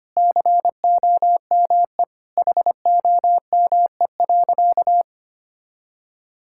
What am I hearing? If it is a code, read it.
Morse code, heard as a keyed tone, about 25 wpm, 700 Hz. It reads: COME HOME.
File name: practice / 25wpm / Sentences with 3-Word Phrases - 25wpm